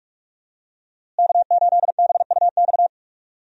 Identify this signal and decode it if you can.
Morse code, heard as a keyed tone, about 44 wpm, 700 Hz. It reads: K8BUX
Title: USA Call Signs - 5 Characters - Repeated 3x - 45wpm